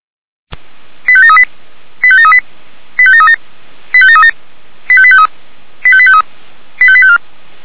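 Five-tone selective call (ZVEI), heard as quick runs of five tones, tones of about 70 ms. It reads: 85738 85738 85737 85737 85853 85853 85854
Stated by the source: ZVEI five-tone call 85727 received by SDR.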